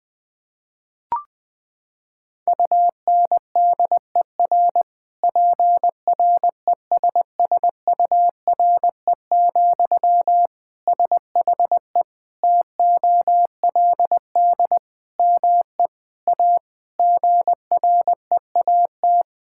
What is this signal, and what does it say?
Morse code, heard as a keyed tone, about 21 wpm, 700 Hz. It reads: UNDER PRESSURE, SHE TOLD ME A GREAT
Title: Sentences from Top 1000 Words - Review Part 6 - 20wpm